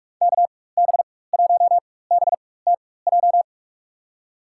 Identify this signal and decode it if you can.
Morse code, heard as a keyed tone, about 42 wpm, spaced as for 21 wpm, 700 Hz. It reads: KB1BTJ
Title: USA Call Signs - 6 Characters - 4x Character Spacing - 45wpm